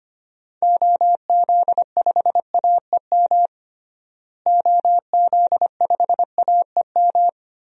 Morse code, heard as a keyed tone, about 25 wpm, 700 Hz. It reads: OZ5AEM OZ5AEM
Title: DXCC-100 Call Signs - 6 Characters - Repeated 3x - 25wpm